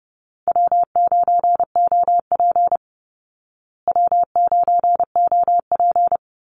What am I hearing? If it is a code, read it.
Morse code, heard as a keyed tone, about 30 wpm, 700 Hz. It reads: W9OP W9OP
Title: Call Signs Repeated 2x - 30wpm